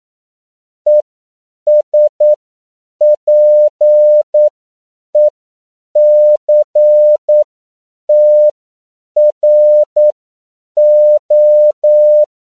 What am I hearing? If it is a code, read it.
Morse code, heard as a keyed tone, about 9 wpm, spaced as for 7 wpm, 600 Hz. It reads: ESPECTRO
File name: CW / 10PPM